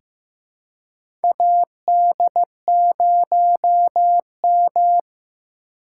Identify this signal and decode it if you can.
Morse code, heard as a keyed tone, about 15 wpm, 700 Hz. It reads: AD0M
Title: USA Call Signs - 2x Character Spacing - 15wpm